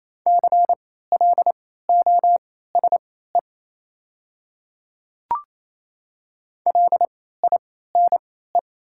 Morse code, heard as a keyed tone, about 28 wpm, spaced as for 16 wpm, 690 Hz. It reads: CLOSE LINE
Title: 3x Character Spacing - Top 300 Words - 28wpm